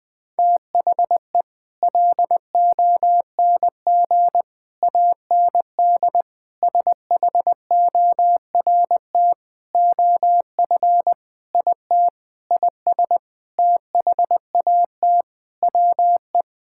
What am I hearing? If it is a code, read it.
Morse code, heard as a keyed tone, about 21 wpm, 700 Hz. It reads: THE LONG AND SHORT OF IT IS THAT WE
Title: Sentences with 8-Word Phrases - 20wpm